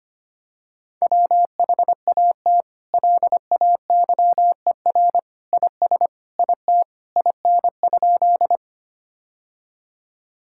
Morse code, heard as a keyed tone, about 25 wpm, 700 Hz. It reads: WHAT LAYER IS IT IN?
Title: Sentences from Top 1600 Words 25wpm